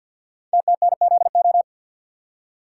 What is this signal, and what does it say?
Morse code, heard as a keyed tone, about 50 wpm, 700 Hz. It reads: TTNZO
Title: Code Groups - 5 Letters - 50wpm